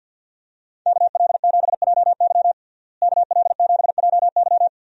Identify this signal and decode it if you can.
Morse code, heard as a keyed tone, about 46 wpm, 700 Hz. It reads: KL7JY KL7JY
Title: DXCC-100 Call Signs - 5 Characters - Repeated 2x - 50wpm